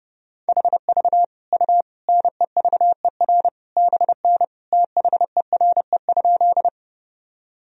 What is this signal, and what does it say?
Morse code, heard as a keyed tone, about 32 wpm, 700 Hz. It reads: HV U NEVER BN THERE?
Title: Sentences from Top 300 Words and Abbreviations 30wpm